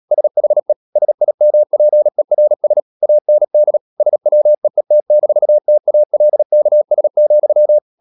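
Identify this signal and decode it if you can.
Morse code, heard as a keyed tone, about 37 wpm, 600 Hz. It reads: SHE SIMPERS AND SWEET-TALKS,
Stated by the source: Morse code practice at 35–39 wpm sending English text of Ken Follett's novel Never, from CW med Gnister.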